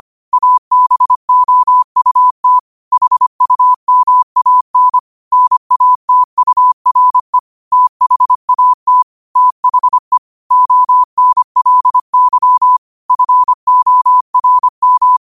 Morse code, heard as a keyed tone, about 25 wpm, 1000 Hz. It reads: ABOUT HUMAN NATURE THAT THE ONLY FORM
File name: qotdfiles / 25wpm